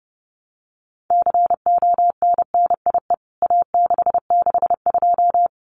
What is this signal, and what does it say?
Morse code, heard as a keyed tone, about 30 wpm, 700 Hz. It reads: CONNIE A662
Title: CWT Exchange 30wpm